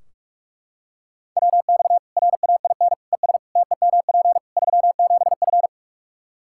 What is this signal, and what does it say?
Morse code, heard as a keyed tone, about 43 wpm, 700 Hz. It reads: WX RAIN ES TEMP 37F
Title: Practice QSO - Exchange 2 - Info - 45wpm